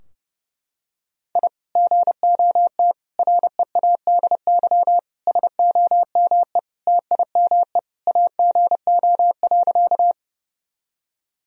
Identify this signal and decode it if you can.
Morse code, heard as a keyed tone, about 32 wpm, 700 Hz. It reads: I GOT READY SOME TIME AGO.
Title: Sentences from Top 1000 Words - Review Part 3 - 30wpm